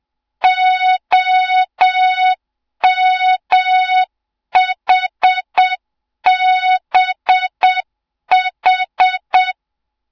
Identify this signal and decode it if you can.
Morse code, heard as a keyed tone, about 7 wpm, 3700 Hz. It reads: OMHBH
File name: Morze-HA2MN / hang / 07 GYAKORLAT